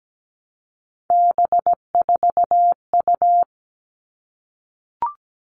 Morse code, heard as a keyed tone, about 17 wpm, 700 Hz. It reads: B4U
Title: All Cut Numbers - 3-Digit 17wpm